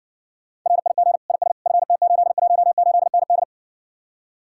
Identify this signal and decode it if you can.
Morse code, heard as a keyed tone, about 67 wpm, 700 Hz. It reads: RIG IS FT818ND